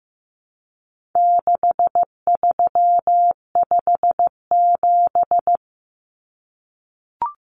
Morse code, heard as a keyed tone, about 15 wpm, 700 Hz. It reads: 6357
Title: Common Cut Numbers - 4-Digit 15wpm